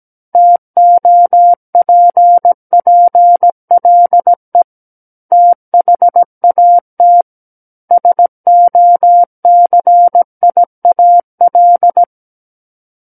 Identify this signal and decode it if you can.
Morse code, heard as a keyed tone, about 17 wpm, 700 Hz. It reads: TOPPLE THAT SOCIAL
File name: Book One / 08-Chapter 8 - 17 WPM